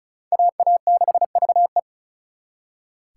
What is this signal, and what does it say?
Morse code, heard as a keyed tone, about 46 wpm, 700 Hz. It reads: AA6VE